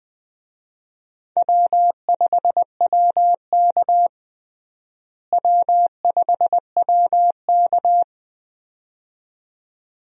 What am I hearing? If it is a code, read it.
Morse code, heard as a keyed tone, about 20 wpm, 700 Hz. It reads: W5WK W5WK
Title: USA Call Signs - 4 Characters - Repeated 2x - 20wpm